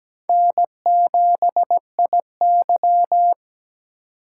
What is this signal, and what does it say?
Morse code, heard as a keyed tone, about 17 wpm, 700 Hz. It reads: N7IY
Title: USA Call Signs - 4x Character Spacing - 17wpm